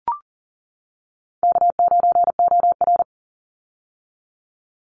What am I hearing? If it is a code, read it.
Morse code, heard as a keyed tone, about 41 wpm, 700 Hz. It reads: K9OR
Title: Call Signs - 40wpm